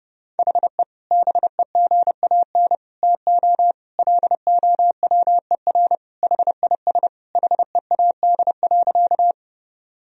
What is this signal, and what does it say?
Morse code, heard as a keyed tone, about 30 wpm, 700 Hz. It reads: HE BEGAN TO LOWER HIS HEAD.
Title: Sentences from Top 1000 Words - Review Part 4 - 30wpm